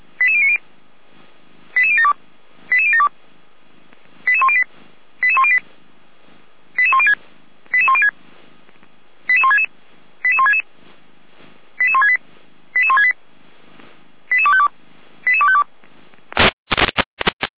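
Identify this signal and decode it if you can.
Five-tone selective call (ZVEI), heard as quick runs of five tones, tones of about 70 ms. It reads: 88090 88072 88072 88108 88108 88106 88106 88166 88166 88168 88168 88252 88252
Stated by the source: ZVEI five-tone call received by SDR, from the Feuerwehr Orlamünde fire brigade server.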